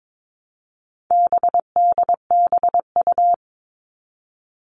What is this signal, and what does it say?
Morse code, heard as a keyed tone, about 22 wpm, 700 Hz. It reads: BDBU
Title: All Cut Numbers - 4-Digit 22wpm